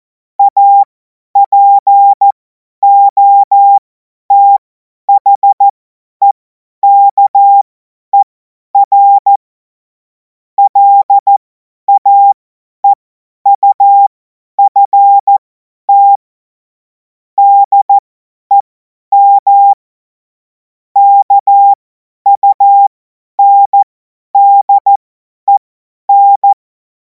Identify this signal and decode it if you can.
Morse code, heard as a keyed tone, about 14 wpm, spaced as for 10 wpm, 800 Hz. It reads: APOTHEKER LAEUFT DEM KUNDEN
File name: Morsekurs / Witze / Witz 19. Mai 2026 50 F70